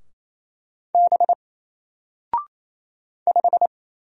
Morse code, heard as a keyed tone, about 28 wpm, 700 Hz. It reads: B 5
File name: Single-Letter-Number-28wpm